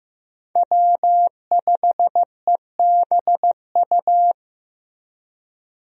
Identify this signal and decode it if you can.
Morse code, heard as a keyed tone, about 15 wpm, 700 Hz. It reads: W5EBU
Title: USA Call Signs - 5 Characters - Repeated 2x - 15wpm